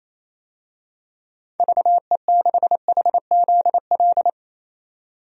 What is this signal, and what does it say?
Morse code, heard as a keyed tone, about 27 wpm, 700 Hz. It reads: VE6HZL
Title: DXCC-100 Call Signs - 6 Characters - 28wpm